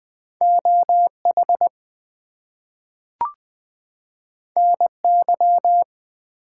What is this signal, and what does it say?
Morse code, heard as a keyed tone, about 20 wpm, 700 Hz. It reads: OH NY
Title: US States and Canadian Provinces 20wpm